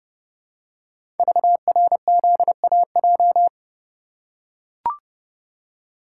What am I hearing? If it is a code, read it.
Morse code, heard as a keyed tone, about 29 wpm, 700 Hz. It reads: VRZAJ
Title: Code Groups - 5 Letters - 30wpm